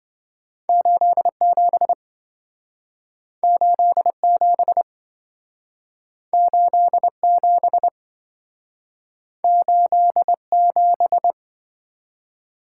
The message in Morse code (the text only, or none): 87 87 87 87